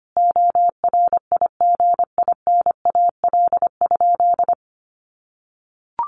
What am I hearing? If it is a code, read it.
Morse code, heard as a keyed tone, about 25 wpm, 700 Hz. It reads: ORIGINAL?
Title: Sentences from Top 900 Words 25wpm